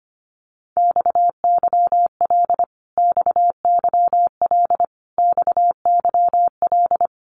Morse code, heard as a keyed tone, about 25 wpm, 700 Hz. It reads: XYL XYL XYL
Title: Abbreviations Repeated 3x - 25wpm